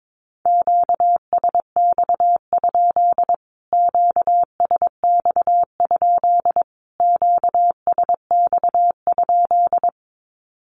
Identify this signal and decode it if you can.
Morse code, heard as a keyed tone, about 22 wpm, 700 Hz. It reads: QSX? QSX? QSX?
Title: Q-Codes Repeated 3x 22wpm